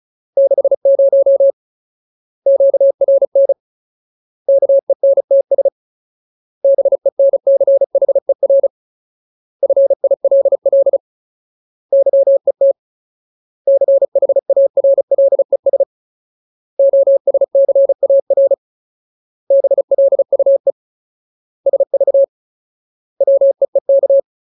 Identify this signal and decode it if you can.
Morse code, heard as a keyed tone, about 34 wpm, 550 Hz. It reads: 60 QRN KENTS BENCHER FILL YET CHARLES OSCAR BLUE SV WEEK
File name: Ultimate Word List / 35WPM Ultimate